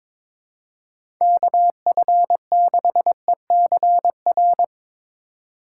Morse code, heard as a keyed tone, about 22 wpm, 700 Hz. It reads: KF6ECR
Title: USA Call Signs - 3x Character Spacing - 22wpm